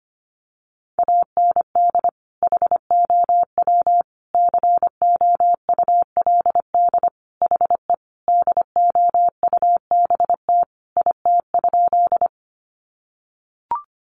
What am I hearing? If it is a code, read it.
Morse code, heard as a keyed tone, about 25 wpm, 700 Hz. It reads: AND HOW COULD HE DOUBT IT?
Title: Sentences from Top 1500 Words 25wpm